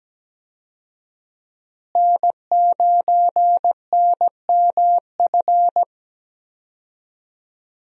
Morse code, heard as a keyed tone, about 17 wpm, 700 Hz. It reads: N9NMF
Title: USA Call Signs - 5 Characters - Repeated 2x - 17wpm